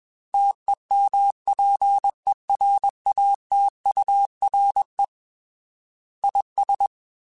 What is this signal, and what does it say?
Morse code, heard as a keyed tone, about 21 wpm, 800 Hz. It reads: TEMPERATURE IS
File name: Weather Sets / Weather Sets-mp3s / Weather 4 24 @20